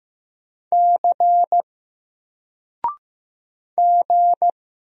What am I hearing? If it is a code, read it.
Morse code, heard as a keyed tone, about 15 wpm, 700 Hz. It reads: C G
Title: Single Character (Letter:Number:Punctuation:Prosign) 15wpm